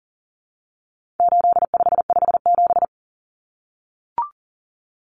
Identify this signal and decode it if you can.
Morse code, heard as a keyed tone, about 40 wpm, 700 Hz. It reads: 8557
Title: Numbers - 4-Digit 40wpm